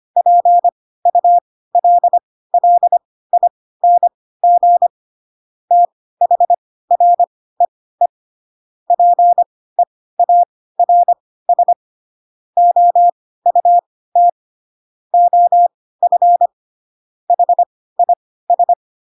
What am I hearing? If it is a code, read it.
Morse code, heard as a keyed tone, about 25 wpm, spaced as for 16 wpm, 700 Hz. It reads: PULLING THREE PEARS OUT OF HIS
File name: Pinocchio / Pinocchio ch7 10 25 ik1hge